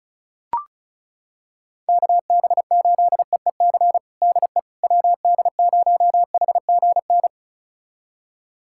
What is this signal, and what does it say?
Morse code, heard as a keyed tone, about 36 wpm, 700 Hz. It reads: KB8EEC DE WD0HGN
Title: USA Callsign DE Callsign - 6 Characters - 35wpm